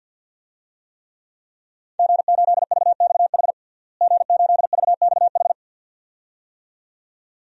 Morse code, heard as a keyed tone, about 53 wpm, 700 Hz. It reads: G8VXH G8VXH
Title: DXCC-100 Call Signs - 5 Characters - Repeated 2x - 50wpm